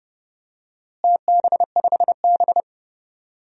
Morse code, heard as a keyed tone, about 30 wpm, 700 Hz. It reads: TB5B
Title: DXCC-100 Call Signs - 4 Characters - Repeated 3x - 30wpm